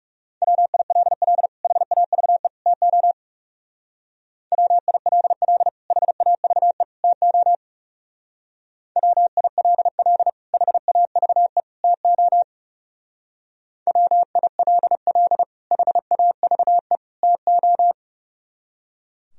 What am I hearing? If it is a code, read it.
Morse code, heard as a keyed tone, about 38 wpm, 700 Hz. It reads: WILL HAVE TO WILL HAVE TO WILL HAVE TO WILL HAVE TO